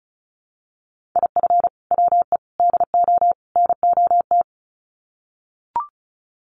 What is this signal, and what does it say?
Morse code, heard as a keyed tone, about 34 wpm, 700 Hz. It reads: IF WE DO NOT
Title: Sets of 4 Words 35wpm